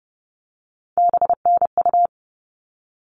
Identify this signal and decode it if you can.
Morse code, heard as a keyed tone, about 30 wpm, 700 Hz. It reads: BNU
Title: All Cut Numbers - 3-Digit 30wpm